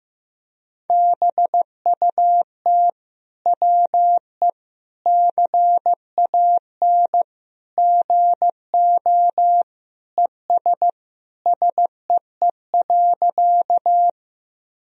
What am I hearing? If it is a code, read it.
Morse code, heard as a keyed tone, about 15 wpm, 700 Hz. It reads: BUT WE CAN GO ES SEE.
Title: Sentences from Top 100 Words and Abbreviations 15wpm